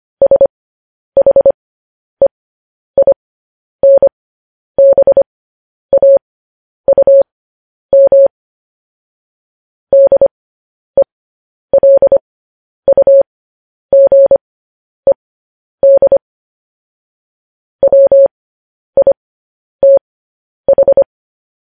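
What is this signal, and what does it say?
Morse code, heard as a keyed tone, about 25 wpm, spaced as for 10 wpm, 550 Hz. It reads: SHEINBAUM DELUGED WITH
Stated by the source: Morse code news headlines practice transmission.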